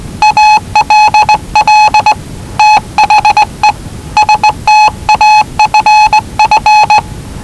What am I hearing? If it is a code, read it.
Morse code, heard as a keyed tone, about 19 wpm, 850 Hz. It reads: ALL THE STAFF
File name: WellingtonRadioZlwCloseDownOnHfAndMf / 01-Zlw 4277khz 417.5 qrt until 0925z